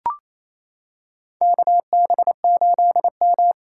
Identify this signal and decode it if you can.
Morse code, heard as a keyed tone, about 27 wpm, 700 Hz. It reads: KB8M
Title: K1USN SST QSO Practice - 28wpm